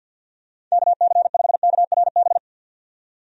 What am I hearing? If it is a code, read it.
Morse code, heard as a keyed tone, about 49 wpm, 700 Hz. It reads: KC5KRB
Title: USA Call Signs - 6 Characters - 50wpm